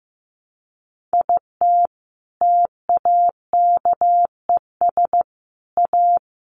Morse code, heard as a keyed tone, about 15 wpm, 700 Hz. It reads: IT TAKES A
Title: Sets of 5 Words Encore 15wpm